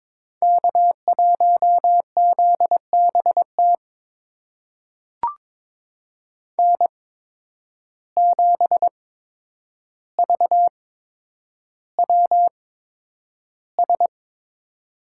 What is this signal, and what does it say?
Morse code, heard as a keyed tone, about 22 wpm, spaced as for 9 wpm, 700 Hz. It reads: K1ZBT N7VWS
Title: USA Call Signs - 5 Characters - 8x Character Spacing - 22wpm